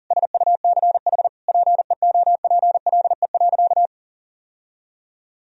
Morse code, heard as a keyed tone, about 37 wpm, 700 Hz. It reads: SUCH PEOPLE.
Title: Sentences from Top 1000 Words - Review Part 3 - 40wpm